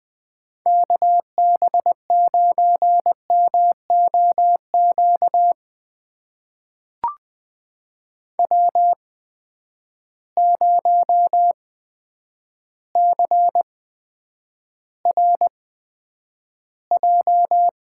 Morse code, heard as a keyed tone, about 20 wpm, spaced as for 9 wpm, 700 Hz. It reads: KB9MOQ W0CRJ
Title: USA Call Signs - 8x Character Spacing - 20wpm